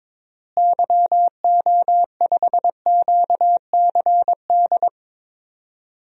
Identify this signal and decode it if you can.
Morse code, heard as a keyed tone, about 21 wpm, 700 Hz. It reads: YO5QCD
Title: DXCC-100 Call Signs - 6 Characters - Repeated 2x - 22wpm